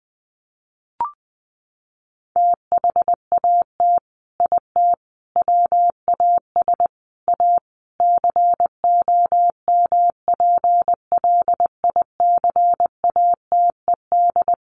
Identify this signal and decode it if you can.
Morse code, heard as a keyed tone, about 20 wpm, 700 Hz. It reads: THAT IT WAS A COMPLICATED
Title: Sentences from Top 1200 Words 20wpm